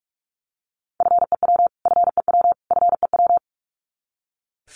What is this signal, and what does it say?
Morse code, heard as a keyed tone, about 44 wpm, 700 Hz. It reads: FEW FEW FEW
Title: Top 300 Words Repeated 45wpm